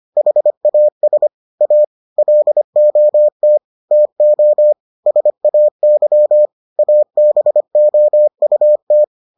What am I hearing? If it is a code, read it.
Morse code, heard as a keyed tone, about 25 wpm, 600 Hz. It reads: HAS A LOT TO SAY ABOUT